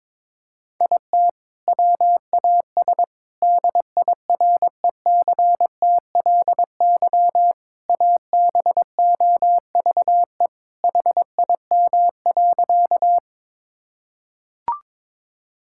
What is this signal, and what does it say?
Morse code, heard as a keyed tone, about 22 wpm, 700 Hz. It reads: IT WAS DIRECTLY ABOVE HIM.
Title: Sentences from Top 1000 Words - Review Part 5 - 22wpm